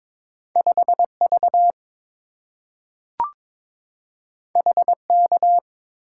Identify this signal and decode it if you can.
Morse code, heard as a keyed tone, about 22 wpm, 690 Hz. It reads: 5V HK
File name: practice / 22wpm / Two Letter-Number Combinations - 22wpm